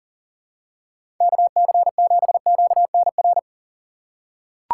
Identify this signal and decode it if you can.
Morse code, heard as a keyed tone, about 40 wpm, 700 Hz. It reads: KC7QNR
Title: USA Call Signs - 2x Character Spacing - 40wpm